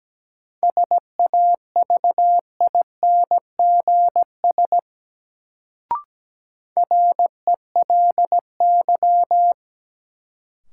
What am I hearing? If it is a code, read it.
Morse code, heard as a keyed tone, about 17 wpm, 700 Hz. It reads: SAVINGS RELY
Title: Sentences from Top 1200 Words - New Word Drill Alt - 17wpm